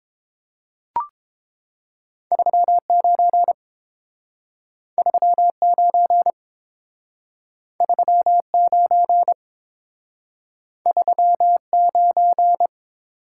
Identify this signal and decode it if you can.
Morse code, heard as a keyed tone, about 30 wpm, 700 Hz. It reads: 39 39 39 39